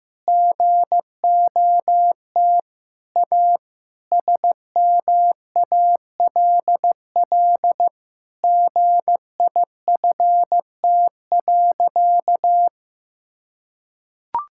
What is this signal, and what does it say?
Morse code, heard as a keyed tone, about 15 wpm, 700 Hz. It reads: GOT A SMALL GIFT.
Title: Sentences from Top 1000 Words - Review Part 6 - 15wpm